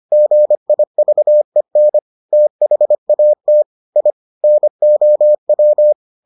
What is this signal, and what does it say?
Morse code, heard as a keyed tone, about 25 wpm, 600 Hz. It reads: GIVEN THAT I NOW